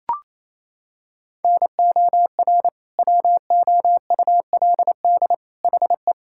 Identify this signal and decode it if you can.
Morse code, heard as a keyed tone, about 28 wpm, 700 Hz. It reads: NOR WOULD HE
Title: Sentences from Top 1600 Words 28wpm